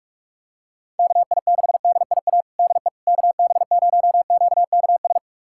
Morse code, heard as a keyed tone, about 46 wpm, 700 Hz. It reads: KI6DIA DE KB0QKS
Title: USA Callsign DE Callsign 45wpm